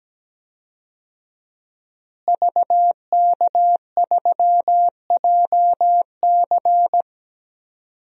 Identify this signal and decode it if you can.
Morse code, heard as a keyed tone, about 17 wpm, 700 Hz. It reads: VK3JC
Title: DXCC-100 Call Signs - 5 Characters - Repeated 2x - 17wpm